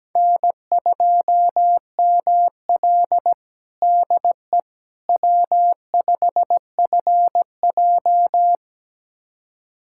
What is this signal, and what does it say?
Morse code, heard as a keyed tone, about 17 wpm, 700 Hz. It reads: N2ML DE W5FJ
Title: USA Callsign DE Callsign - 4 Characters - 17wpm